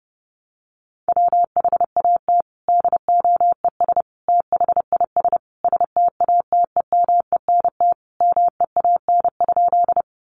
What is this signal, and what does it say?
Morse code, heard as a keyed tone, about 30 wpm, 700 Hz. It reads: WHAT DOES THIS STATEMENT MEAN?
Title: Sentences from Top 1200 Words 30wpm